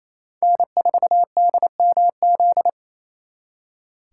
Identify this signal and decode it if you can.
Morse code, heard as a keyed tone, about 36 wpm, 700 Hz. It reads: N4DMZ